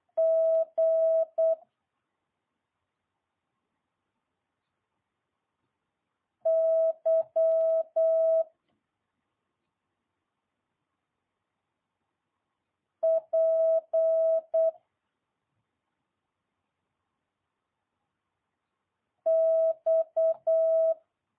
Morse code, heard as a keyed tone, about 8 wpm, 650 Hz. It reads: G Y P X
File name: jh8cbh